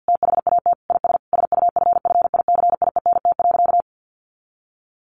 Morse code, heard as a keyed tone, about 50 wpm, 700 Hz. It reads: THAT IS SUFFICIENT.
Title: Sentences from Top 1400 Words 50wpm